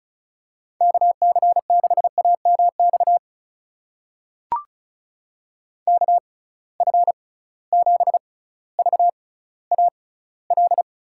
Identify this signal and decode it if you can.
Morse code, heard as a keyed tone, about 36 wpm, spaced as for 18 wpm, 700 Hz. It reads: KC6AMX KF7VAL
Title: USA Call Signs - 6x Character Spacing - 35wpm